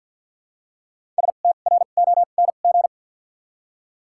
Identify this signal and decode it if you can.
Morse code, heard as a keyed tone, about 53 wpm, 700 Hz. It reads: S T R O N G
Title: 2x Character Spacing - Top 500 Words - 50wpm